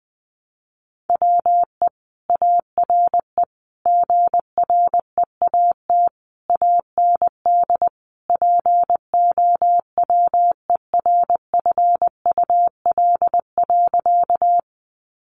Morse code, heard as a keyed tone, about 20 wpm, 700 Hz. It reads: WE ARE GREAT AND POWERFUL.
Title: Sentences from Top 1300 Words 20wpm